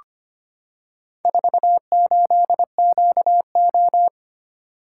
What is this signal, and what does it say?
Morse code, heard as a keyed tone, about 25 wpm, 700 Hz. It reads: 48QO
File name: practice / 25wpm / Four Letter-Number Combinations - 25wpm